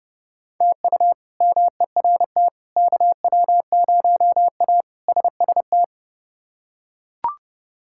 Morse code, heard as a keyed tone, about 30 wpm, 700 Hz. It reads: TU MERT KW0A SST
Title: K1USN SST QSO Practice - 30wpm